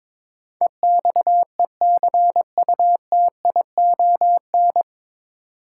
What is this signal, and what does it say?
Morse code, heard as a keyed tone, about 21 wpm, 700 Hz. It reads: EXECUTION
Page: Morse Code Ninja practice 22wpm